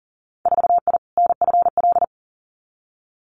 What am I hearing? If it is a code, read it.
Morse code, heard as a keyed tone, about 37 wpm, 700 Hz. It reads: 4I NFL
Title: Winter Field Day Exchanges 40wpm